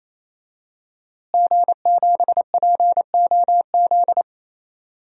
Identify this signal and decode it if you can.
Morse code, heard as a keyed tone, about 27 wpm, 700 Hz. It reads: G7POZ
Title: DXCC-100 Call Signs - 5 Characters - Repeated 3x - 28wpm